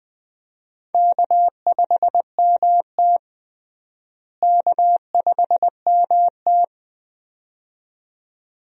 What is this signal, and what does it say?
Morse code, heard as a keyed tone, about 20 wpm, 700 Hz. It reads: K5MT K5MT
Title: USA Call Signs - 4 Characters - Repeated 2x - 20wpm